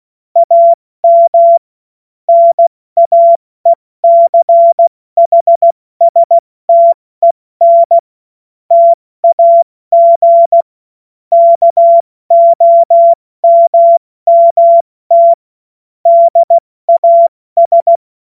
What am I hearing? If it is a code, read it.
Morse code, heard as a keyed tone, about 16 wpm, 680 Hz. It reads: AM NAECHSTEN TAG KOMMT DAS
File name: Morsekurs / Witze / Witz 25. Mai 2021, 70 F80